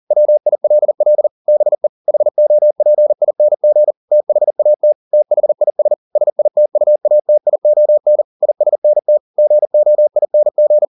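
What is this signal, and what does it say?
Morse code, heard as a keyed tone, about 40 wpm, 600 Hz. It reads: WILL BE HOPING THAT THIS SITUATION ISNT GOING
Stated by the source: Morse code at 40 wpm, practice text, CW med Gnister.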